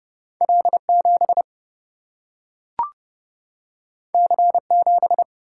Morse code, heard as a keyed tone, about 30 wpm, 700 Hz. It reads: L 7 C 7
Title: Single-Letter Single-Number 30wpm